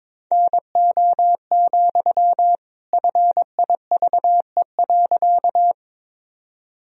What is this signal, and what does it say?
Morse code, heard as a keyed tone, about 22 wpm, 700 Hz. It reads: NO, FIVE.
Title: Sentences from Top 1000 Words - Review Part 2 - 22wpm